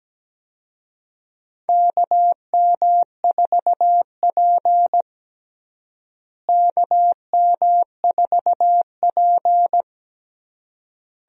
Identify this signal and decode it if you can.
Morse code, heard as a keyed tone, about 17 wpm, 700 Hz. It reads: KM4P KM4P
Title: USA Call Signs - 4 Characters - Repeated 3x - 17wpm